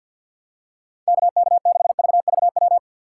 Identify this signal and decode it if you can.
Morse code, heard as a keyed tone, about 48 wpm, 690 Hz. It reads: KK6VVW
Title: USA Call Signs - 6 Characters - 3x Character Spacing - 50wpm